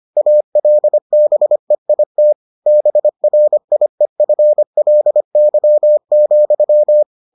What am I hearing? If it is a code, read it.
Morse code, heard as a keyed tone, about 25 wpm, 600 Hz. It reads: ALBEIT BRIEFLY,